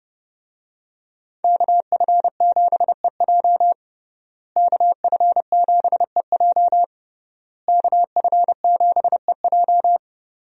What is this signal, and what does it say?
Morse code, heard as a keyed tone, about 32 wpm, 700 Hz. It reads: KF7EJ KF7EJ KF7EJ
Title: USA Call Signs - 5 Characters - Repeated 3x - 30wpm